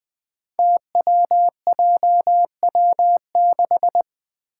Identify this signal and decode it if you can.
Morse code, heard as a keyed tone, about 20 wpm, 700 Hz. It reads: TWJW6
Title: Five Challenging Letter-Number Combinations 20wpm